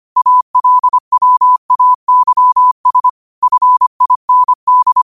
Morse code, heard as a keyed tone, about 25 wpm, 1000 Hz. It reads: ALWAYS FIND